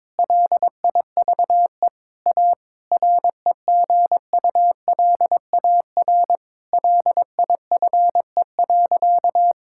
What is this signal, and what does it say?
Morse code, heard as a keyed tone, about 22 wpm, 700 Hz. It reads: LIVE A REGULAR LIFE.
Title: Sentences from Top 1000 Words - Review Part 6 - 22wpm